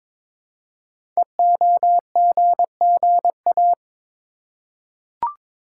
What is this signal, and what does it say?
Morse code, heard as a keyed tone, about 22 wpm, 700 Hz. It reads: EOGGA
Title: Code Groups - 5 Letters - 22wpm